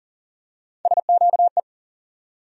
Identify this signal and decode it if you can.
Morse code, heard as a keyed tone, about 40 wpm, 700 Hz. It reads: SQE
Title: USA Call Sign Postfixes - 3 Letters 40wpm